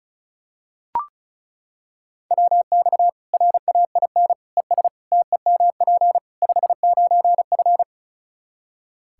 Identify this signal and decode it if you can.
Morse code, heard as a keyed tone, about 34 wpm, 700 Hz. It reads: WX RAIN ES TEMP 59F
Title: Practice QSO - Exchange 2 - Info - 35wpm